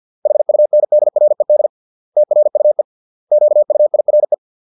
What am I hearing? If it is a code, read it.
Morse code, heard as a keyed tone, about 50 wpm, 600 Hz. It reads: HUNDRED TRUE QUIRE